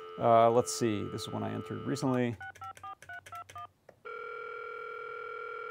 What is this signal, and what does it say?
DTMF, heard as telephone touch tones, tones of about 120 ms, with speech. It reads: 654321